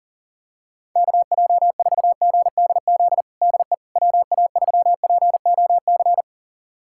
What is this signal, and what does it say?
Morse code, heard as a keyed tone, about 37 wpm, 700 Hz. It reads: KJ4GDZ DE WA3POC
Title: USA Callsign DE Callsign 40wpm